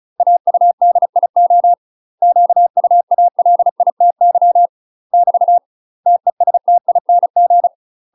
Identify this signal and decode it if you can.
Morse code, heard as a keyed tone, about 36 wpm, 700 Hz. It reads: AUDIO QUALITY <BT> TESTING
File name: UK MASTER QSOs TRAINING / 35WPM / UK MASTER QSOs 0013